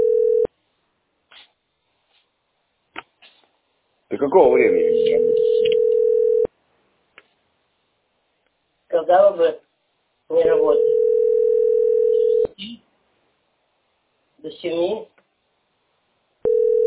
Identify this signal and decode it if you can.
Ringback tone (North America), 440 + 480 Hz, with speech.
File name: 9bbbfac1-2e24-4730-b62e